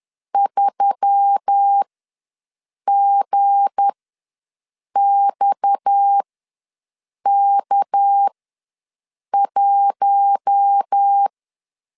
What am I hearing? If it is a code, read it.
Morse code, heard as a keyed tone, about 10 wpm, spaced as for 6 wpm, 800 Hz. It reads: 3GXK1
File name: LEZIONI / 7 GF9Z / 7 G ws 10wpm 3843